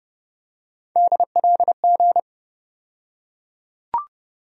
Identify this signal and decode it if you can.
Morse code, heard as a keyed tone, about 31 wpm, 700 Hz. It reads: DLG
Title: USA Call Sign Postfixes - 3 Letters 30wpm